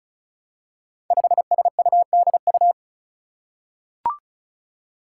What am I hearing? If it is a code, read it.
Morse code, heard as a keyed tone, about 36 wpm, 700 Hz. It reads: 5SUDU
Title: Five Challenging Letter-Number Combinations 35wpm